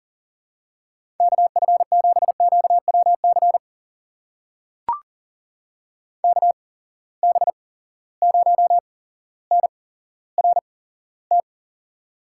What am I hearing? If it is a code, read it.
Morse code, heard as a keyed tone, about 37 wpm, spaced as for 17 wpm, 700 Hz. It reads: KF7QWC KB0NRT
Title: USA Call Signs - 6 Characters - 8x Character Spacing - 40wpm